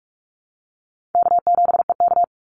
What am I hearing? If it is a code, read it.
Morse code, heard as a keyed tone, about 44 wpm, 700 Hz. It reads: K7EK
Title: Easy Call Signs - 45wpm